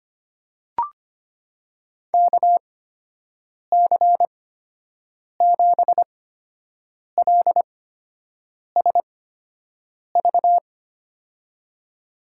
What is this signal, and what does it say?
Morse code, heard as a keyed tone, about 25 wpm, spaced as for 7 wpm, 700 Hz. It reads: KC7LSV
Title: USA Call Signs - 8x Character Spacing - 25wpm